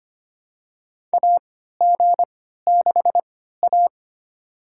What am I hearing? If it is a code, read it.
Morse code, heard as a keyed tone, about 25 wpm, spaced as for 14 wpm, 700 Hz. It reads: AG6A
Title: USA Call Signs - 4 Characters - 3x Character Spacing - 25wpm